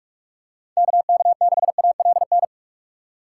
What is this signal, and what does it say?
Morse code, heard as a keyed tone, about 46 wpm, 700 Hz. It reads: KK6ALN
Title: USA Call Signs - 6 Characters - 2x Character Spacing - 45wpm